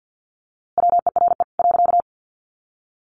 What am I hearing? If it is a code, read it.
Morse code, heard as a keyed tone, about 50 wpm, 700 Hz. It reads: WERE .
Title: A Sound Of Thunder 50wpm